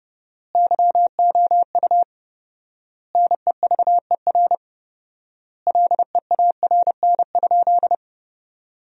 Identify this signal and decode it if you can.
Morse code, heard as a keyed tone, about 32 wpm, 700 Hz. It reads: YOU NEVER LEARN?
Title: 4x Word-Spacing - Sentences from Top 300 Words - 30wpm